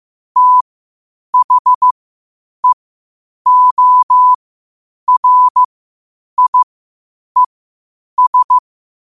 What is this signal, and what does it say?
Morse code, heard as a keyed tone, about 15 wpm, spaced as for 8 wpm, 1000 Hz. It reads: THEORIES
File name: qotdfiles / 8wpm